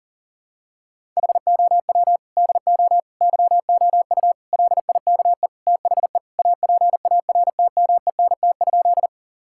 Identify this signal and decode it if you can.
Morse code, heard as a keyed tone, about 40 wpm, 700 Hz. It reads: HOW DO YOU LIKE THE APARTMENT?
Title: Sentences from Top 1000 Words - Review Part 6 - 40wpm